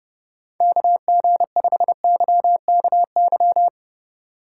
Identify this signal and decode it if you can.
Morse code, heard as a keyed tone, about 28 wpm, 700 Hz. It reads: KG5YKY